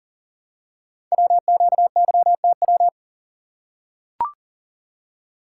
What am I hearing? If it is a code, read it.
Morse code, heard as a keyed tone, about 41 wpm, 700 Hz. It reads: WQYTW
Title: Code Groups - 5 Letters - 40wpm